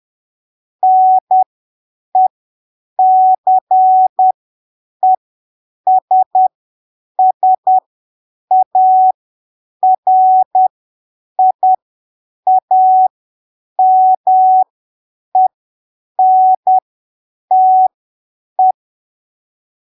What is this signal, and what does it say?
Morse code, heard as a keyed tone, about 10 wpm, spaced as for 7 wpm, 750 Hz. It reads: NECESSARIAMENTE
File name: arquivos / texto / texto 30 10ppm